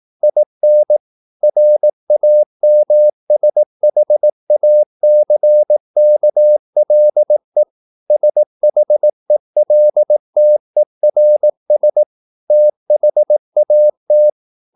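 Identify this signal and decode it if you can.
Morse code, heard as a keyed tone, about 18 wpm, 600 Hz. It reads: IN RAMSHACKLE SHELTERS THAT
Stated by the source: Morse practice at 18 wpm, English text, from CW med Gnister.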